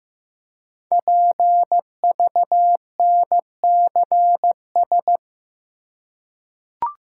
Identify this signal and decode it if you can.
Morse code, heard as a keyed tone, about 15 wpm, 700 Hz. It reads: PVNCS
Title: Code Groups - 5 Letter:Numbers - 15wpm